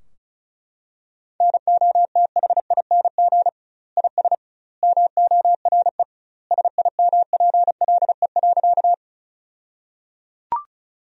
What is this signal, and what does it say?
Morse code, heard as a keyed tone, about 36 wpm, 700 Hz. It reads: NOTHING IS MORE SIMPLE.
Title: 2x Word-Spacing - Sentences from Top 500 Words - 35wpm